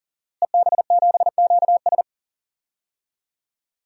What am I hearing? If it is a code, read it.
Morse code, heard as a keyed tone, about 38 wpm, 700 Hz. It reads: EB7QS
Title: DXCC-100 Call Signs - 5 Characters - Repeated 3x - 40wpm